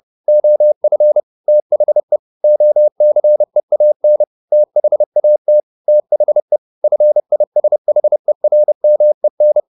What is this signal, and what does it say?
Morse code, heard as a keyed tone, about 30 wpm, 600 Hz. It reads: OF THE OCEAN THAT THE FISHERMEN
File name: the old man and the sea / the old man and the sea 1 30wpm